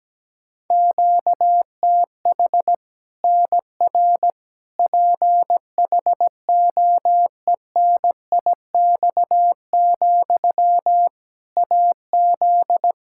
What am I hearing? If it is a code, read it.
Morse code, heard as a keyed tone, about 17 wpm, 700 Hz. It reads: QTH NR PHOENIX, AZ
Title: K1USN SST QSO Practice 17wpm